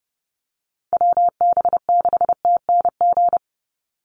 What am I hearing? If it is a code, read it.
Morse code, heard as a keyed tone, about 30 wpm, 700 Hz. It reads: WB6TNG
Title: USA Call Signs 30wpm